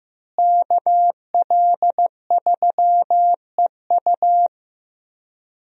Morse code, heard as a keyed tone, about 15 wpm, 700 Hz. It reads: KL3EU
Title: USA Call Signs - 5 Characters - Repeated 3x - 15wpm